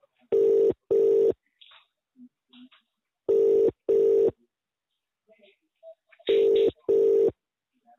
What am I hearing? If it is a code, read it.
Ringback tone (UK), 400 + 450 Hz, with speech.